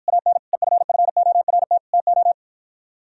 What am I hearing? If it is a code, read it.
Morse code, heard as a keyed tone, about 53 wpm, 700 Hz. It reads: AN EFFORT TO